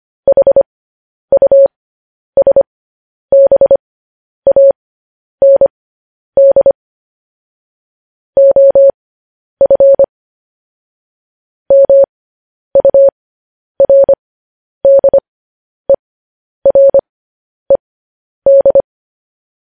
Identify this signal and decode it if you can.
Morse code, heard as a keyed tone, about 25 wpm, spaced as for 10 wpm, 550 Hz. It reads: HUSBAND OF MURDERED
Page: news headlines in morse code at 10 wpm